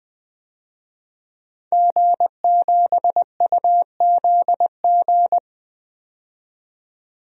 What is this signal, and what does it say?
Morse code, heard as a keyed tone, about 20 wpm, 700 Hz. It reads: G7UZG
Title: DXCC-100 Call Signs - 5 Characters - 20wpm